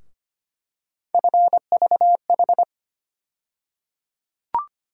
Morse code, heard as a keyed tone, about 25 wpm, 700 Hz. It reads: FVH
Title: USA Call Sign Postfixes - 3 Letters 25wpm